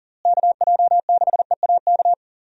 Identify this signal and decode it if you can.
Morse code, heard as a keyed tone, about 40 wpm, 700 Hz. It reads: KJ6EAK
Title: USA Call Signs - 6 Characters - Repeated 2x - 40wpm